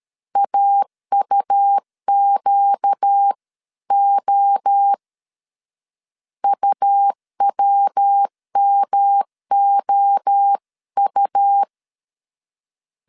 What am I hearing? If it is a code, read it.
Morse code, heard as a keyed tone, about 13 wpm, 800 Hz. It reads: AUQO UWMOU
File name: CW / DET / 1 9 1500C S B 12wpm 3819